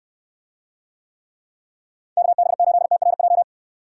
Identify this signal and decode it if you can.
Morse code, heard as a keyed tone, about 64 wpm, 700 Hz. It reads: KB8TRJ